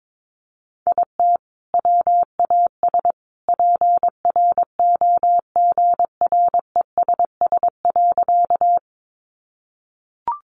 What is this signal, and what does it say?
Morse code, heard as a keyed tone, about 22 wpm, 700 Hz. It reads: IT WAS PROGRESS.
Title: Sentences from Top 1400 Words 22wpm